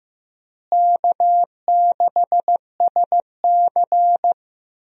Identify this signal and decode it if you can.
Morse code, heard as a keyed tone, about 15 wpm, 700 Hz. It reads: K6SC
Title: USA Call Signs - 4 Characters - Repeated 2x - 15wpm